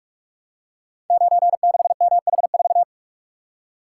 Morse code, heard as a keyed tone, about 42 wpm, 700 Hz. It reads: 96MH4
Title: Code Groups - 5 Letter:Numbers - 45wpm